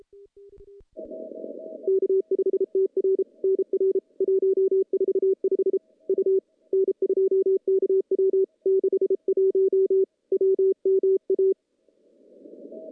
Morse code, heard as a keyed tone, about 33 wpm, 380 Hz. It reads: K5TR NR 145 U N2KW 61 WMA
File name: contests / CW-SS-k5tr- -0100